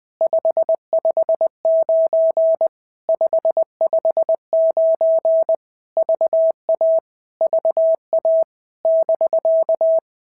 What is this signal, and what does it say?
Morse code, heard as a keyed tone, about 19 wpm, 650 Hz. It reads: 559 559 VA VA <BK>
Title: POTA QSO Practice - 20wpm